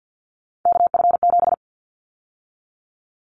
Morse code, heard as a keyed tone, about 50 wpm, 700 Hz. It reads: KF7
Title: Callsign Prefixes for Top 25 DXCC - 50wpm